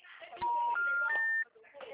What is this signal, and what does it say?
Special information tone, with speech.
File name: out- -2002- - - .51598